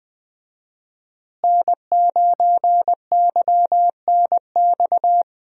Morse code, heard as a keyed tone, about 21 wpm, 700 Hz. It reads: N9YNX
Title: USA Call Signs - 5 Characters - Repeated 3x - 20wpm